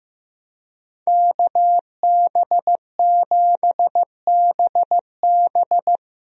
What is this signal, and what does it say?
Morse code, heard as a keyed tone, about 15 wpm, 700 Hz. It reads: KB7BB
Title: USA Call Signs - 5 Characters - Repeated 2x - 15wpm